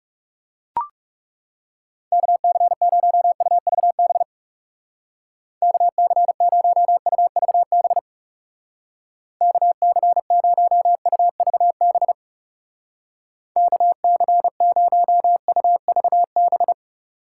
Morse code, heard as a keyed tone, about 36 wpm, 700 Hz. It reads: KC0UVB KC0UVB KC0UVB KC0UVB
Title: Speed Racing - USA 2x3 Call Signs 30wpm